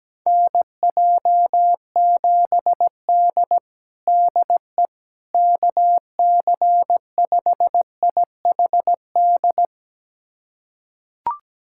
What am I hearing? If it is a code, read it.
Morse code, heard as a keyed tone, about 17 wpm, 700 Hz. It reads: NJ7D DE KC5IHD
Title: USA Callsign DE Callsign 17wpm